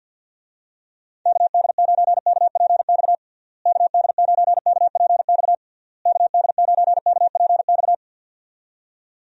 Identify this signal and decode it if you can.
Morse code, heard as a keyed tone, about 50 wpm, 700 Hz. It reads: KD9KPX KD9KPX KD9KPX
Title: USA Call Signs - 6 Characters - Repeated 3x - 50wpm